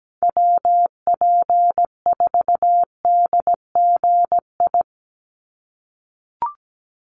Morse code, heard as a keyed tone, about 17 wpm, 700 Hz. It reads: WP4DGI
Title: USA Call Signs 17wpm